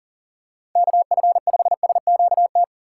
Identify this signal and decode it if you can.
Morse code, heard as a keyed tone, about 40 wpm, 700 Hz. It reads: KF5SQT
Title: USA Call Signs - 6 Characters - 2x Character Spacing - 40wpm